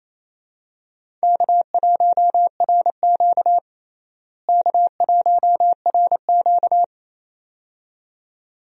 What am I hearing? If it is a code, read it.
Morse code, heard as a keyed tone, about 28 wpm, 700 Hz. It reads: K1RQ K1RQ
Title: USA Call Signs - 4 Characters - Repeated 2x - 28wpm